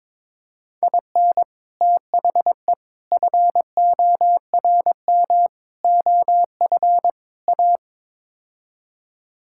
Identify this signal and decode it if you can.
Morse code, heard as a keyed tone, about 22 wpm, 700 Hz. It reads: IN THE FORM OF A
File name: practice / 22wpm / Build A Phrase - 5-Word Phrases - 22wpm